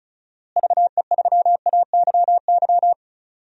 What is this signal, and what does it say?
Morse code, heard as a keyed tone, about 36 wpm, 700 Hz. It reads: VE3AYY
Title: DXCC-100 Call Signs - 6 Characters - Repeated 3x - 35wpm